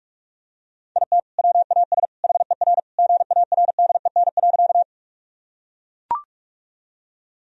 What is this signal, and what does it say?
Morse code, heard as a keyed tone, about 43 wpm, 700 Hz. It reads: IT WAS HER GARDEN.
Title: Sentences from Top 1000 Words - Review Part 4 - 45wpm